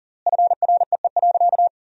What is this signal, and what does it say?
Morse code, heard as a keyed tone, about 40 wpm, 700 Hz. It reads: FREE.
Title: 3x Word-Spacing - Sentences from Top 500 Words - 40wpm